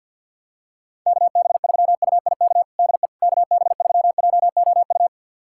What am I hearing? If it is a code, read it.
Morse code, heard as a keyed tone, about 51 wpm, 700 Hz. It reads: KB3UIK DE KB3JOU
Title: USA Callsign DE Callsign - 6 Characters - 50wpm